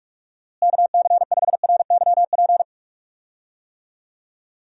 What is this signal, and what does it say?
Morse code, heard as a keyed tone, about 42 wpm, 700 Hz. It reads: KC5RYP
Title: USA Call Signs - 6 Characters - Repeated 2x - 45wpm